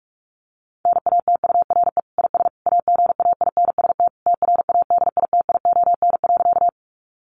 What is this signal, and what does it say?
Morse code, heard as a keyed tone, about 44 wpm, 700 Hz. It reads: NATURE IS AGAINST TRADITION.
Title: Sentences from Top 1500 Words 45wpm